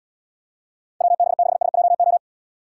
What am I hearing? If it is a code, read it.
Morse code, heard as a keyed tone, about 74 wpm, 700 Hz. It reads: WB6SJO